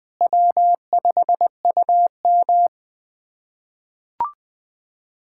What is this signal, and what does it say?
Morse code, heard as a keyed tone, about 20 wpm, 700 Hz. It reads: W5UM
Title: USA Callsign DE Callsign - 4 Characters - 20wpm